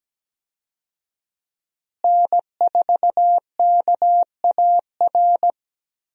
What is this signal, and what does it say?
Morse code, heard as a keyed tone, about 17 wpm, 700 Hz. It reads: N4KAR
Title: USA Call Signs - 5 Characters - Repeated 2x - 17wpm